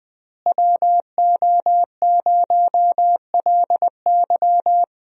Morse code, heard as a keyed tone, about 19 wpm, 700 Hz. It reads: WO0LY
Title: USA Call Signs - 5 Characters - Repeated 3x - 20wpm